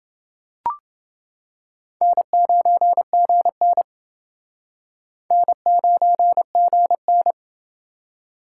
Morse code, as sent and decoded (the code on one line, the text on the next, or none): -. ----. --. -. / -. ----. --. -.
N9GN N9GN